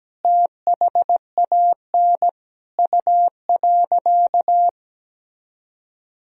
Morse code, heard as a keyed tone, about 17 wpm, 700 Hz. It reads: THAN U.
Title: Sentences from Top 400 Words and Abbreviations 17wpm